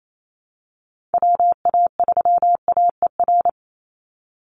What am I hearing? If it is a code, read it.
Morse code, heard as a keyed tone, about 28 wpm, 700 Hz. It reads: WA3AER
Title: Call Signs Repeated 3x - 28wpm